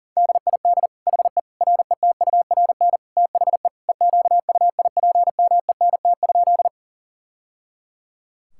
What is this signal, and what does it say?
Morse code, heard as a keyed tone, about 40 wpm, 700 Hz. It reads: DID HE RETURN THE EQUIPMENT?
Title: Sentences from Top 1000 Words - Review Part 5 - 40wpm